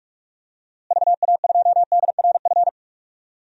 Morse code, heard as a keyed tone, about 45 wpm, 700 Hz. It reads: VA2DRF
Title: DXCC-100 Call Signs - 6 Characters - Repeated 3x - 45wpm